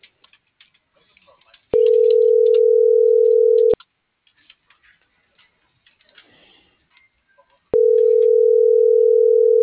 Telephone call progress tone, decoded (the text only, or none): ringback tone